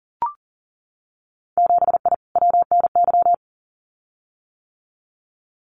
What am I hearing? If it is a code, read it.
Morse code, heard as a keyed tone, about 40 wpm, 700 Hz. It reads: 7I WNY
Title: Winter Field Day Exchanges Encore 40wpm